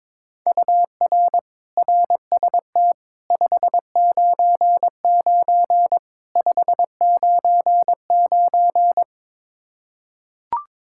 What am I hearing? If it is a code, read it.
Morse code, heard as a keyed tone, about 22 wpm, 700 Hz. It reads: UR RST 599 599
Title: K1USN SST QSO Practice 22wpm